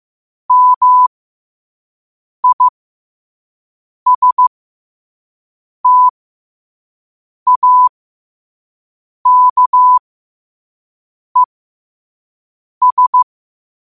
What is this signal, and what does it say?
Morse code, heard as a keyed tone, about 15 wpm, spaced as for 5 wpm, 1000 Hz. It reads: MISTAKES
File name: qotdfiles / 5wpm